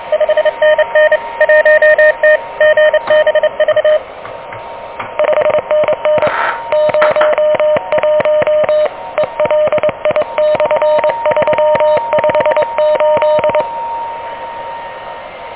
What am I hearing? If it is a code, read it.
Morse code, heard as a keyed tone, about 25 wpm, 590 Hz, with noise. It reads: 5NN 1T GBV 5NN 01 ELI/358
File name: OK QRP 2023 / OK QRP 2023